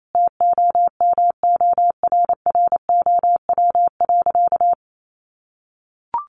Morse code, heard as a keyed tone, about 28 wpm, 700 Hz. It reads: TOMORROW.
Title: Sentences from Top 1100 Words 28wpm